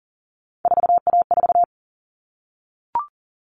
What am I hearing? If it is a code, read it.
Morse code, heard as a keyed tone, about 40 wpm, 700 Hz. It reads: 4A4
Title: All Cut Numbers - 3-Digit 40wpm